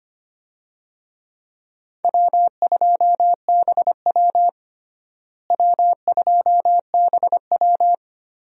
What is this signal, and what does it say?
Morse code, heard as a keyed tone, about 25 wpm, 700 Hz. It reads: W2BW W2BW
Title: USA Call Signs - 4 Characters - Repeated 3x - 25wpm